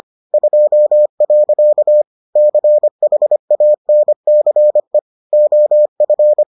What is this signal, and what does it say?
Morse code, heard as a keyed tone, about 25 wpm, 600 Hz. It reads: 2. CHANCE OF